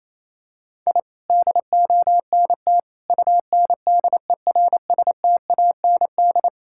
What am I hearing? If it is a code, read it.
Morse code, heard as a keyed tone, about 28 wpm, 700 Hz. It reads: I DONT UNDERSTAND
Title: Sentences from Top 1000 Words - Review Part 5 - 28wpm